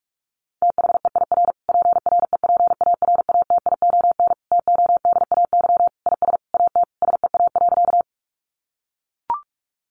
Morse code, heard as a keyed tone, about 44 wpm, 700 Hz. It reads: THEIR PREPARATION TODAY IS AT SEA.
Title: Sentences from Top 1400 Words 45wpm